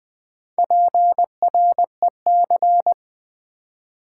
Morse code, heard as a keyed tone, about 20 wpm, 700 Hz. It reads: PREC
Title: Build A Word - 4 Letters At A Time - 8-Letter Words - 20wpm